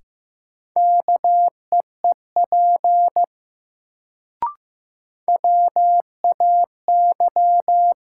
Morse code, heard as a keyed tone, about 15 wpm, 700 Hz. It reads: KEEP WAY
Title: Sets of 2 Words - New Word Drill - 15wpm